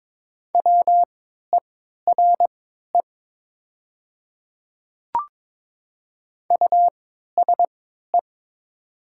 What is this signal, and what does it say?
Morse code, heard as a keyed tone, about 22 wpm, spaced as for 12 wpm, 700 Hz. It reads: WERE USE
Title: 3x Character Spacing - Top 100 Words - 22wpm